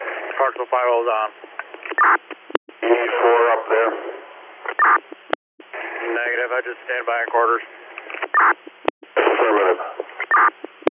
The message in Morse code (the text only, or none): none